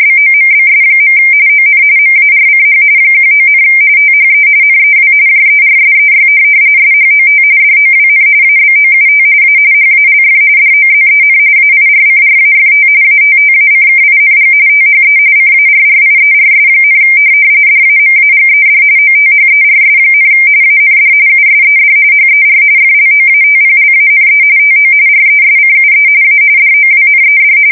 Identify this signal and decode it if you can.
RTTY, about 45.45 baud. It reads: BOMBS. ON MARCH 19, THE NEW JERSEY SHOT DOWN ABOUT A HALF DOZEN AIRCRAFT ATTACKING THE CARRIERS, DAMAGED OTHERS, AND USED HER FLOATPLANE TO RESCUE A DOWNED
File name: fsk170 / USS-New-Jersey